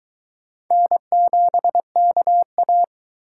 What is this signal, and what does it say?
Morse code, heard as a keyed tone, about 22 wpm, 700 Hz. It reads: N7KA